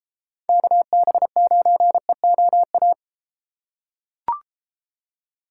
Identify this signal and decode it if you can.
Morse code, heard as a keyed tone, about 31 wpm, 700 Hz. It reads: KB9EOA